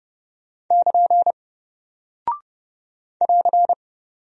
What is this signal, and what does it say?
Morse code, heard as a keyed tone, about 30 wpm, 700 Hz. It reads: <KN> <AR>
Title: Single Character (Letter:Number:Punctuation:Prosign) Alt 30wpm